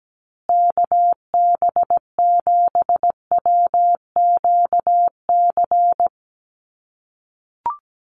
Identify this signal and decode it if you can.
Morse code, heard as a keyed tone, about 17 wpm, 700 Hz. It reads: KB7WQC
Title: USA Call Signs Repeated 2x 17wpm